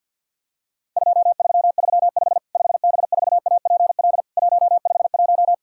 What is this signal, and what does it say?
Morse code, heard as a keyed tone, about 50 wpm, 700 Hz. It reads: 2335 5B4APL 151
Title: Worked All Europe DX Contest - QTC Exchange with Repeats - 50wpm